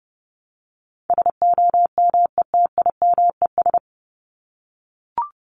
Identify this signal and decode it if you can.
Morse code, heard as a keyed tone, about 30 wpm, 700 Hz. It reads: SOMETIMES
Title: Top 501-750 Words 30wpm